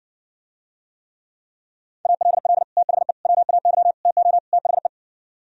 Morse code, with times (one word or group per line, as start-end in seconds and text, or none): 2.1-2.6 ALL
2.8-3.1 THE
3.3-3.9 WAY
4.1-4.4 TO
4.5-4.9 THE